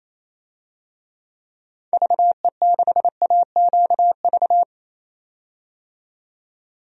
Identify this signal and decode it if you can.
Morse code, heard as a keyed tone, about 28 wpm, 700 Hz. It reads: VE6AQV
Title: DXCC-100 Call Signs - 6 Characters - 28wpm